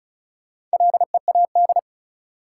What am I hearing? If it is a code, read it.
Morse code, heard as a keyed tone, about 34 wpm, 700 Hz. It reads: LEAD
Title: Sentences from Top 500 Words - New Word Drill - 35wpm